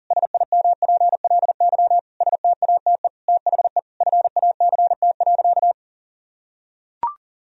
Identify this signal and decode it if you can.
Morse code, heard as a keyed tone, about 37 wpm, 700 Hz. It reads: SIMPLY STATE THE FACT.
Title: Sentences from Top 1000 Words - Review Part 4 - 40wpm